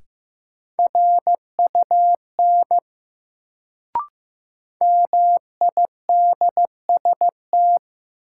Morse code, heard as a keyed tone, about 15 wpm, 700 Hz. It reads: RUN MIDST
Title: Sets of 4 Words - New Word Drill - 15wpm